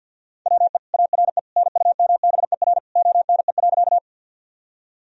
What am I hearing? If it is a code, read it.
Morse code, heard as a keyed tone, about 50 wpm, 700 Hz. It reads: WE ARE NUMBER ONE.
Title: Sentences from Top 1000 Words - Review Part 1 - 50wpm